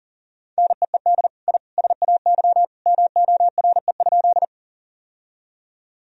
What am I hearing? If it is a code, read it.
Morse code, heard as a keyed tone, about 40 wpm, 700 Hz. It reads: NEED I SAY MORE?
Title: Sentences from Top 1000 Words - Review Part 1 - 40wpm